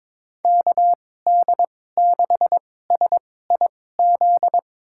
Morse code, heard as a keyed tone, about 21 wpm, spaced as for 16 wpm, 700 Hz. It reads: KD6SIZ
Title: USA Call Signs - 6 Characters - 2x Character Spacing - 22wpm